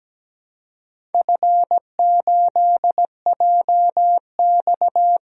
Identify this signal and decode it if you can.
Morse code, heard as a keyed tone, about 17 wpm, 700 Hz. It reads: F8JX
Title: DXCC-100 Call Signs - 4 Characters - Repeated 3x - 17wpm